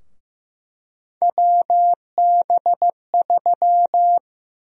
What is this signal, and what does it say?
Morse code, heard as a keyed tone, about 15 wpm, 700 Hz. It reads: WB3
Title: USA Call Sign Prefixes - Repeated 2x - 15wpm